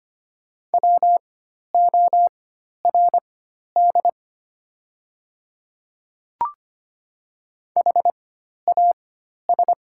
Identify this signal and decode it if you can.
Morse code, heard as a keyed tone, about 25 wpm, spaced as for 12 wpm, 700 Hz. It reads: WORD HAS
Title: 4x Character Spacing - Top 100 Words - 25wpm